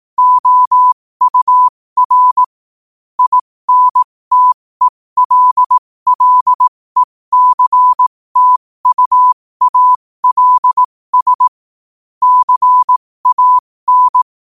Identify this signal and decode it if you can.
Morse code, heard as a keyed tone, about 18 wpm, spaced as for 15 wpm, 1000 Hz. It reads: OUR INTELLECTUALS CAN
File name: qotdfiles / 15wpm